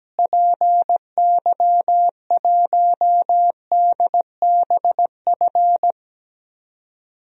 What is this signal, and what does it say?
Morse code, heard as a keyed tone, about 17 wpm, 700 Hz. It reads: PY1DBF
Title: DXCC-100 Call Signs - 6 Characters - Repeated 2x - 17wpm